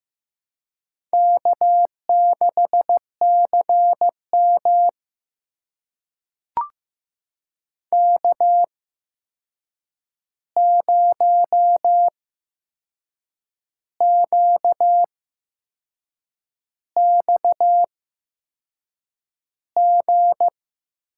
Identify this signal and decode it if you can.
Morse code, heard as a keyed tone, about 15 wpm, spaced as for 6 wpm, 700 Hz. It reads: K6CM K0QXG
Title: USA Call Signs - 8x Character Spacing - 15wpm